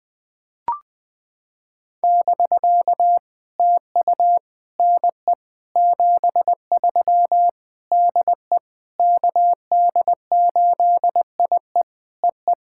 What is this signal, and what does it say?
Morse code, heard as a keyed tone, about 20 wpm, 700 Hz. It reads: <BK> TU NE 73 DE KD8IE EE
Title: POTA QSO Practice - 20wpm